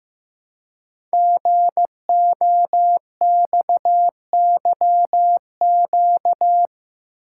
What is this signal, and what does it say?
Morse code, heard as a keyed tone, about 15 wpm, 700 Hz. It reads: GOXYQ
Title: Code Groups - 5 Letter:Numbers - 15wpm